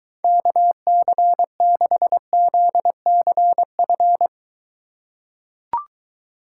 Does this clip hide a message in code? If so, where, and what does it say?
Yes: Morse code from 0.2 to 4.3 s, heard as a keyed tone, about 21 wpm, 700 Hz. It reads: KC6ZCF